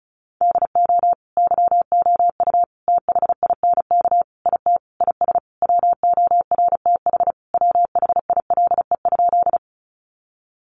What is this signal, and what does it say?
Morse code, heard as a keyed tone, about 34 wpm, 700 Hz. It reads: DO YOU THINK IT IS WORTH WHILE?
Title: Sentences from Top 800 Words 35wpm